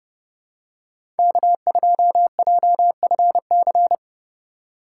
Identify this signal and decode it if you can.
Morse code, heard as a keyed tone, about 30 wpm, 700 Hz. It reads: K2JFC
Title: USA Call Signs - 3x Character Spacing - 30wpm